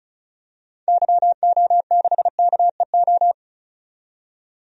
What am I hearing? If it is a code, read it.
Morse code, heard as a keyed tone, about 36 wpm, 700 Hz. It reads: YO6KEO
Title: DXCC-100 Call Signs - 6 Characters - 35wpm